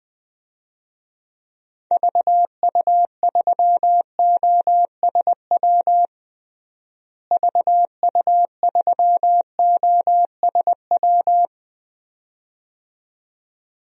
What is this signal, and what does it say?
Morse code, heard as a keyed tone, about 21 wpm, 700 Hz. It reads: VU3OSW VU3OSW
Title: DXCC-100 Call Signs - 6 Characters - Repeated 2x - 20wpm